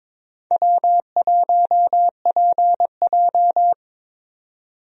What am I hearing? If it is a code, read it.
Morse code, heard as a keyed tone, about 22 wpm, 700 Hz. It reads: W1PJ
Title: USA Call Signs - 4 Characters - 22wpm